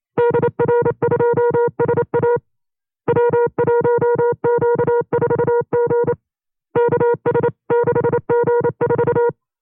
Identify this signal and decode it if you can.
Morse code, heard as a keyed tone, about 28 wpm, 480 Hz. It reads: DR2SA W1Q4G KS6G4
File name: 25WPM